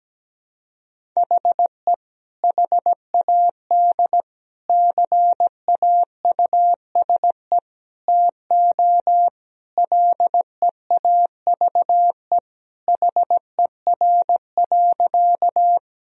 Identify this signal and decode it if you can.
Morse code, heard as a keyed tone, about 17 wpm, 700 Hz. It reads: HE HAD CAUSE TO LEAVE HER.
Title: Sentences from Top 1000 Words - Review Part 1 - 17wpm